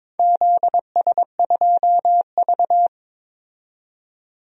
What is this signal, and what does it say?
Morse code, heard as a keyed tone, about 22 wpm, 700 Hz. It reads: ZS2V
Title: DXCC-100 Call Signs - 4 Characters - 22wpm